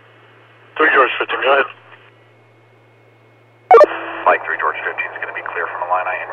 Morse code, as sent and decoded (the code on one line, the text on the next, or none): none